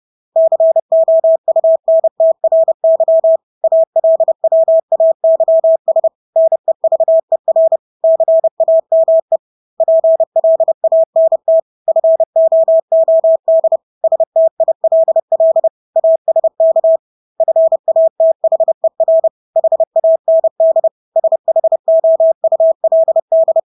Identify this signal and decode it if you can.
Morse code, heard as a keyed tone, about 30 wpm, 650 Hz. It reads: COUNTRY ALWAYS NEVER CAME PLANT FOOD STILL ASK FATHER HAND SHOULD
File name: Engelske ord 30-30 wpm